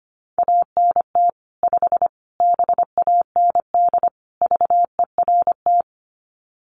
Morse code, heard as a keyed tone, about 25 wpm, 700 Hz. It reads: ANT 5 BAND VERT
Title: Common QSO Phrases 25wpm